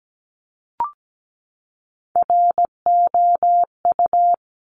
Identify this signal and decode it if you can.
Morse code, heard as a keyed tone, about 17 wpm, 700 Hz. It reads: ROU
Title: Top Three Letter Combinations in Words 17wpm